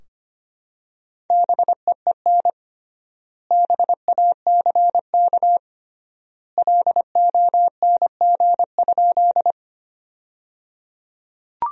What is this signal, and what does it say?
Morse code, heard as a keyed tone, about 25 wpm, 700 Hz. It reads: BEEN BACK LONG?
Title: 3x Word-Spacing - Sentences from Top 200 Words - 25wpm